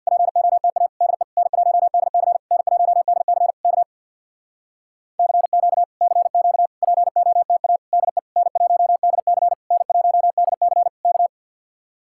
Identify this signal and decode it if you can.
Morse code, heard as a keyed tone, about 61 wpm, 700 Hz. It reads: POTA DE N1DC N1DC K CQ CQ POTA DE N1DC N1DC K
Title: Speed-Racing Common QSO Phrases - 50wpm